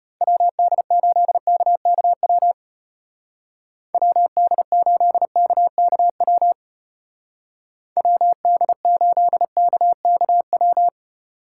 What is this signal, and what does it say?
Morse code, heard as a keyed tone, about 30 wpm, 700 Hz. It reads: WD8KKW WD8KKW WD8KKW
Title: Speed Racing - USA 2x3 Call Signs 25wpm